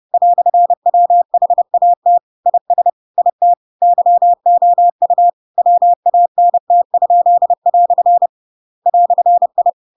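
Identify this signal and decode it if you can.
Morse code, heard as a keyed tone, about 30 wpm, 700 Hz. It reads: "WHAT IS IT YOU WANT?" "I
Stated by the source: Chapter 2 of The Adventures of Pinocchio read as a book in Morse code (CW) at 30 wpm, by ik1hge.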